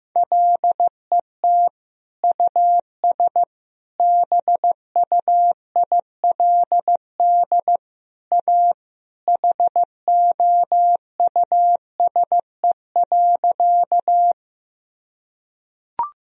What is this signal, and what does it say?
Morse code, heard as a keyed tone, about 16 wpm, 700 Hz. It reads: LET US BUILD A HOUSE.
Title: Sentences from Top 1000 Words - Review Part 1 - 15wpm